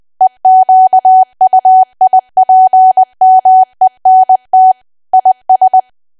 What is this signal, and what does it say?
Morse code, heard as a keyed tone, about 20 wpm, 750 Hz. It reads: EQUIPMENT IS
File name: cwa / W1AW-Level3 / 20WPM